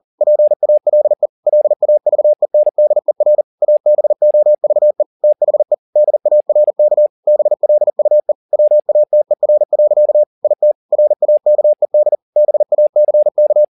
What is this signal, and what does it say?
Morse code, heard as a keyed tone, about 40 wpm, 600 Hz. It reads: PALE LAVENDER ABOVE THE DARK BLUE WATER. IT RAKED BACK
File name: the old man and the sea 2 40wpm